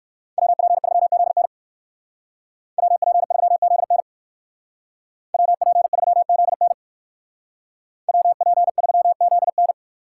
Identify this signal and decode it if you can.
Morse code, heard as a keyed tone, about 59 wpm, 700 Hz. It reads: WP3ZN WP3ZN WP3ZN WP3ZN